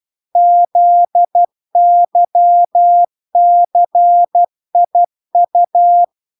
Morse code, heard as a keyed tone, about 12 wpm, 700 Hz. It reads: ZYCIU